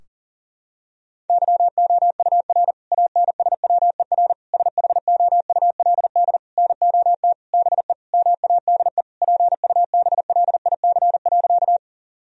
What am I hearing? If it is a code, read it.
Morse code, heard as a keyed tone, about 37 wpm, 700 Hz. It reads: YOUR ANSWER SHOULD NOT BE MADE PUBLIC.
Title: Sentences from Top 1000 Words - Review Part 3 - 40wpm